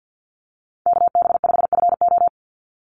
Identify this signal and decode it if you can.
Morse code, heard as a keyed tone, about 50 wpm, 700 Hz. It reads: KB5FO
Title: USA Call Signs 50wpm